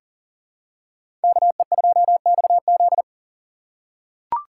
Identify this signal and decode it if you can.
Morse code, heard as a keyed tone, about 40 wpm, 700 Hz. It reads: KE2XZ
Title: USA Call Signs - 6x Character Spacing - 40wpm